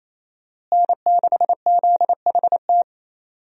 Morse code, heard as a keyed tone, about 27 wpm, 700 Hz. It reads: N6ZHT
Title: USA Call Signs - 5 Characters - Repeated 3x - 28wpm